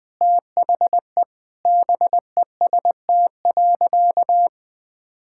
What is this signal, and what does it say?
Morse code, heard as a keyed tone, about 20 wpm, 700 Hz. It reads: THE BEST.
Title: Sentences with 7-Word Phrases - 20wpm